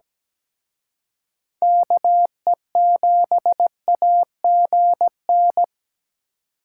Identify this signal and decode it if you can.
Morse code, heard as a keyed tone, about 20 wpm, 700 Hz. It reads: KE7AGN
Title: Speed Racing - USA 2x3 Call Signs 17wpm